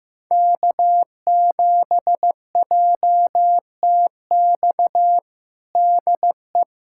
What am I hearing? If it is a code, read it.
Morse code, heard as a keyed tone, about 16 wpm, 700 Hz. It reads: K7JTX DE
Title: USA Callsign DE Callsign - 5 Characters - 15wpm